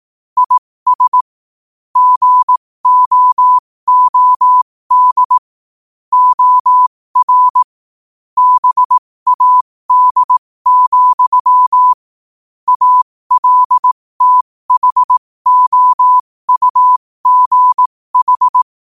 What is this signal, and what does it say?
Morse code, heard as a keyed tone, about 18 wpm, spaced as for 15 wpm, 1000 Hz. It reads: IS GOOD OR BAD, ALTHOUGH
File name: qotdfiles / 15wpm